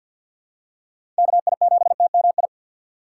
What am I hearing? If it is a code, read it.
Morse code, heard as a keyed tone, about 48 wpm, 700 Hz. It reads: KI7TMI
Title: USA Call Signs - 4x Character Spacing - 50wpm